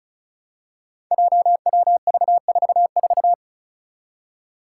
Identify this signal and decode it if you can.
Morse code, heard as a keyed tone, about 34 wpm, 700 Hz. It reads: JWV44
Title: Five Challenging Letter-Number Combinations 35wpm